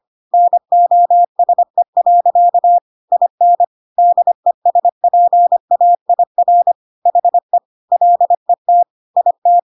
Morse code, heard as a keyed tone, about 25 wpm, 700 Hz. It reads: NOSE. IN DESPAIR HE LET IT
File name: Pinocchio / Pinocchio ch3 25 25 ik1hge